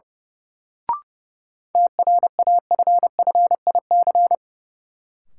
Morse code, heard as a keyed tone, about 29 wpm, 700 Hz. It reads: TRAFFIC
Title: Sentences from Top 1300 Words - New Word Drill Alt - 30wpm